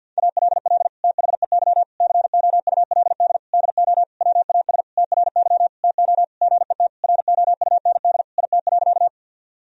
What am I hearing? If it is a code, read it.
Morse code, heard as a keyed tone, about 52 wpm, 700 Hz. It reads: ALL THEY COULD DO WAS TRY TO GET ROUND IT.
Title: Sentences from Top 1000 Words - Review Part 1 - 50wpm